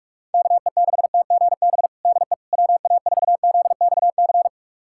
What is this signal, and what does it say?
Morse code, heard as a keyed tone, about 46 wpm, 700 Hz. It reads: KE6TGB DE WA4ZXC
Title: USA Callsign DE Callsign 45wpm